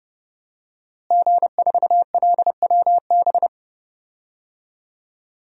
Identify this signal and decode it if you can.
Morse code, heard as a keyed tone, about 32 wpm, 700 Hz. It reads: G4LWB
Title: DXCC-100 Call Signs - 5 Characters - Repeated 3x - 30wpm